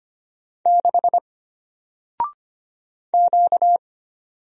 Morse code, heard as a keyed tone, about 25 wpm, 700 Hz. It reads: 6 Q
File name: Single-Letter-Number-25wpm